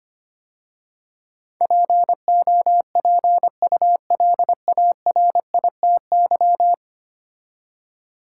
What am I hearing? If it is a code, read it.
Morse code, heard as a keyed tone, about 25 wpm, 700 Hz. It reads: POPULARITY
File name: 10 Letter Words - First 3 Letters Given - 25wpm